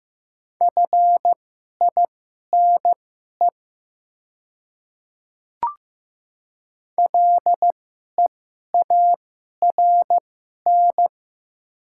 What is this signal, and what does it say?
Morse code, heard as a keyed tone, about 15 wpm, spaced as for 11 wpm, 690 Hz. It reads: FINE LEARN
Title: 2x Character Spacing - Top 500 Words - 15wpm